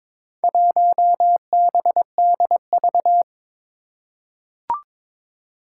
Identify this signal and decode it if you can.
Morse code, heard as a keyed tone, about 22 wpm, 700 Hz. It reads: 1BDV
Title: Four Challenging Letter-Number Combinations 22wpm